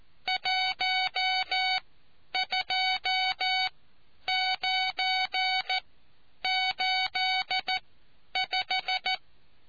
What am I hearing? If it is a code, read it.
Morse code, heard as a keyed tone, about 14 wpm, spaced as for 9 wpm, 750 Hz. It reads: 12985